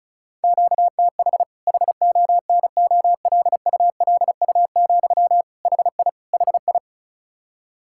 Morse code, heard as a keyed tone, about 36 wpm, 700 Hz. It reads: QTH HONOLULU, HI HI
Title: Practice QSO - Exchange 1 - Beginning - 35wpm